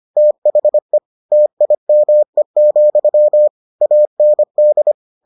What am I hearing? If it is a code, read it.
Morse code, heard as a keyed tone, about 25 wpm, 600 Hz. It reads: THE TIME, AND